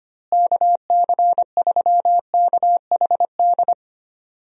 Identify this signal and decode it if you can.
Morse code, heard as a keyed tone, about 25 wpm, 700 Hz. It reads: KC3KHD
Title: USA Call Signs - 6 Characters - Repeated 2x - 25wpm